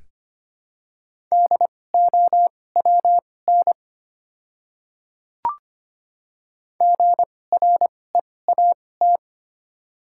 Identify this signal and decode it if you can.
Morse code, heard as a keyed tone, about 25 wpm, spaced as for 18 wpm, 700 Hz. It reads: DOWN GREAT
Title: 2x Character Spacing - Top 500 Words - 25wpm